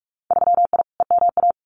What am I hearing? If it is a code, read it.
Morse code, heard as a keyed tone, about 44 wpm, 700 Hz. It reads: 3I EMA
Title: Winter Field Day Exchanges Encore 45wpm